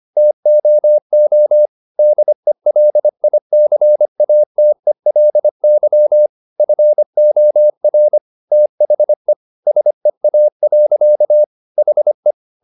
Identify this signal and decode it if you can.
Morse code, heard as a keyed tone, about 25 wpm, 600 Hz. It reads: TOO DELICATELY FOR THE SEA. HE
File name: the old man and the sea part 1 25wpm